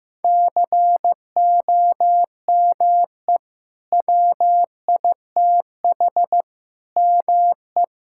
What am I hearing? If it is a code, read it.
Morse code, heard as a keyed tone, about 16 wpm, 700 Hz. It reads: COME WITH ME
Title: Sentences from Top 1600 Words 15wpm